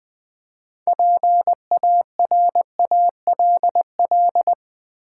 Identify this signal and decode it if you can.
Morse code, heard as a keyed tone, about 20 wpm, 700 Hz. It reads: PARALL
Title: Build A Word - 2 Letters At A Time - 8-Letter Words - 20wpm